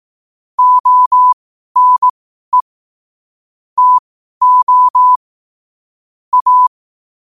Morse code, heard as a keyed tone, about 18 wpm, spaced as for 12 wpm, 1000 Hz. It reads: ONE TO A
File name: qotdfiles / 12wpm